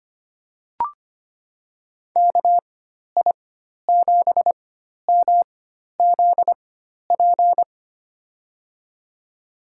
Morse code, heard as a keyed tone, about 25 wpm, spaced as for 12 wpm, 700 Hz. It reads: KI7MZP
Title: USA Call Signs - 4x Character Spacing - 25wpm